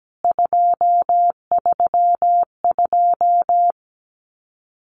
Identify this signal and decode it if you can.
Morse code, heard as a keyed tone, about 17 wpm, 700 Hz. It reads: 232
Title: CWT Exchange 17wpm